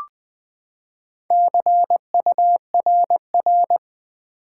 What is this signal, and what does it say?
Morse code, heard as a keyed tone, about 20 wpm, 700 Hz. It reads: CURR
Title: Build A Word - 4 Letters At A Time - 10-Letter Words - 20wpm